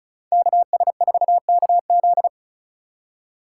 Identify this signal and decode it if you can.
Morse code, heard as a keyed tone, about 36 wpm, 700 Hz. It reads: KS4KZ
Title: USA Call Signs - 4x Character Spacing - 35wpm